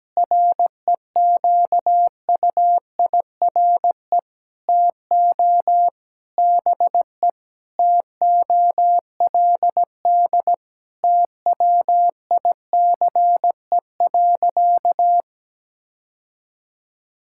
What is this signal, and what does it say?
Morse code, heard as a keyed tone, about 17 wpm, 700 Hz. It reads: REQUIRE TO BE TOLD TWICE.
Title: Sentences from Top 1600 Words 17wpm